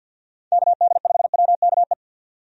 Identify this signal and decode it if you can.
Morse code, heard as a keyed tone, about 51 wpm, 700 Hz. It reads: KD5WKE
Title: USA Call Signs - 6 Characters - 50wpm